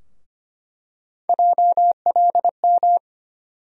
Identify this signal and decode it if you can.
Morse code, heard as a keyed tone, about 25 wpm, 700 Hz. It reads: JLM
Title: USA Call Sign Postfixes - 3 Letters 25wpm